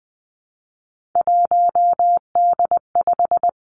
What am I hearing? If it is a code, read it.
Morse code, heard as a keyed tone, about 20 wpm, 700 Hz. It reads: 1D5
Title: Three Challenging Letter-Number Combinations 20wpm